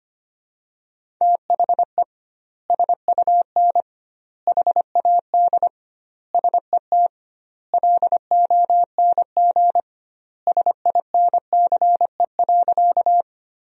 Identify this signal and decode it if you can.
Morse code, heard as a keyed tone, about 25 wpm, 700 Hz. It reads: THE SUN HAD SET LONG SINCE.
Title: 2x Word-Spacing - Sentences from Top 300 Words - 25wpm